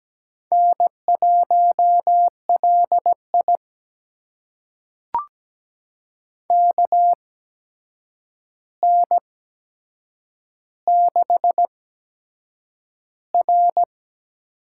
Morse code, heard as a keyed tone, about 17 wpm, spaced as for 7 wpm, 700 Hz. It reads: N1LI KN6R
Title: USA Call Signs - 4 Characters - 8x Character Spacing - 17wpm